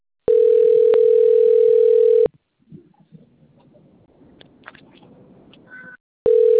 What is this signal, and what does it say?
Ringback tone (North America), 440 + 480 Hz.
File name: rg-1600-unknown- - - .49775